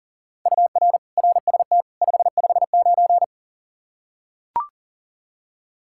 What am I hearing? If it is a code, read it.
Morse code, heard as a keyed tone, about 40 wpm, 700 Hz. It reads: UR RST 559
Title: Practice QSO - Exchange 1 - Beginning - 40wpm